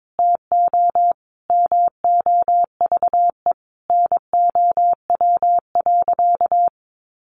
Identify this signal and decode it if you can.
Morse code, heard as a keyed tone, about 22 wpm, 700 Hz. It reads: TO MOVE NOW.
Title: Sentences from Top 1200 Words 22wpm